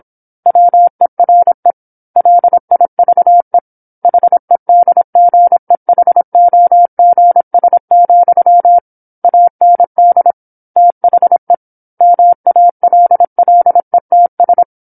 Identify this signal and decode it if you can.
Morse code, heard as a keyed tone, about 27 wpm, 700 Hz. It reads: WERE LIVE HEDGEHOGS, AND THE MALLETS
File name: 08-Chapter-08 26 WPM